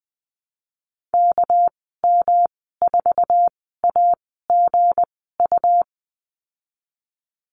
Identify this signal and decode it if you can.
Morse code, heard as a keyed tone, about 20 wpm, spaced as for 14 wpm, 700 Hz. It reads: KM4AGU
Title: 200 Lesson 32g - Letter k - Call Signs 20wpm